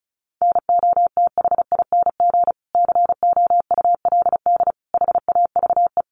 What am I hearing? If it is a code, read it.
Morse code, heard as a keyed tone, about 34 wpm, 700 Hz. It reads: NOTHING COULD HAVE
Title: Sentences from Top 1300 Words 35wpm